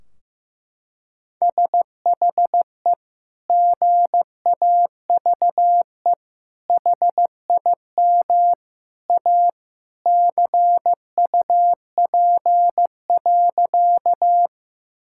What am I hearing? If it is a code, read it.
Morse code, heard as a keyed tone, about 15 wpm, 700 Hz. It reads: SHE GAVE HIM A CUP.
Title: Sentences from Top 1600 Words 15wpm